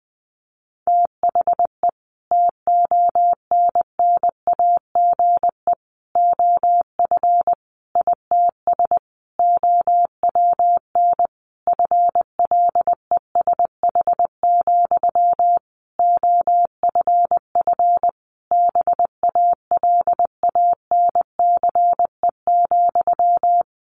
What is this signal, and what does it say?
Morse code, heard as a keyed tone, about 19 wpm, 700 Hz. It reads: THE TONNAGE OF ITS OWN FLESH, OFF BALANCE,
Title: A Sound Of Thunder 20wpm